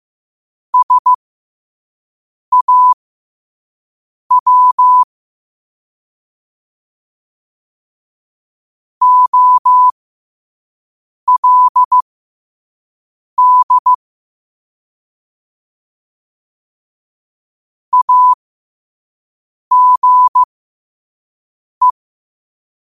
Morse code, heard as a keyed tone, about 15 wpm, spaced as for 5 wpm, 1000 Hz. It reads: SAW OLD AGE
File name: qotdfiles / 5wpm